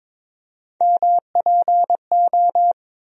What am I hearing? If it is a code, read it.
Morse code, heard as a keyed tone, about 22 wpm, 700 Hz. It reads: MPO
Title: USA Call Sign Postfixes - 3 Letters 22wpm